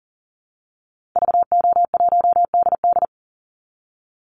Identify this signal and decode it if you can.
Morse code, heard as a keyed tone, about 40 wpm, 700 Hz. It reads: VO1DD
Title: Call Signs Repeated 3x - 40wpm